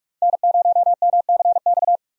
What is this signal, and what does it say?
Morse code, heard as a keyed tone, about 46 wpm, 700 Hz. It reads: N0MCX
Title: USA Call Signs - 5 Characters - 45wpm